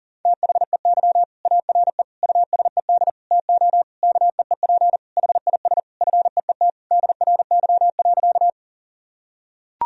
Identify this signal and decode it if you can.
Morse code, heard as a keyed tone, about 37 wpm, 700 Hz. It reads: THEY ARE USED TO KEEP HIS FEET DRY.
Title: Sentences from Top 1000 Words - Review Part 3 - 40wpm